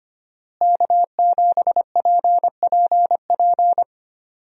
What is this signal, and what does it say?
Morse code, heard as a keyed tone, about 25 wpm, 700 Hz. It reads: K7PPP
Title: USA Call Signs - 5 Characters - Repeated 2x - 25wpm